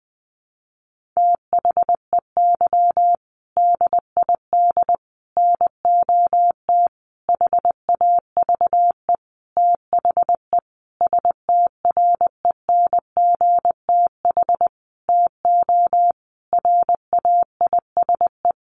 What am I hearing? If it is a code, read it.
Morse code, heard as a keyed tone, about 20 wpm, 700 Hz. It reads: THEY DID NOT HAVE THE STRENGTH TO RAISE
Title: Sentences from Top 1400 Words 20wpm